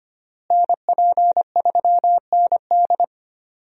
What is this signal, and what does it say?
Morse code, heard as a keyed tone, about 25 wpm, 700 Hz. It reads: NP3ND
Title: USA Call Signs - 5 Characters - Repeated 3x - 25wpm